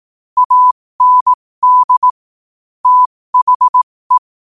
Morse code, heard as a keyed tone, about 18 wpm, spaced as for 15 wpm, 1000 Hz. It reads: AND THE
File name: qotdfiles / 15wpm